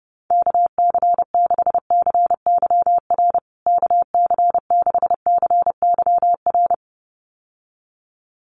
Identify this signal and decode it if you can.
Morse code, heard as a keyed tone, about 30 wpm, 700 Hz. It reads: KC6CYR KC6CYR
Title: USA Call Signs Repeated 2x 30wpm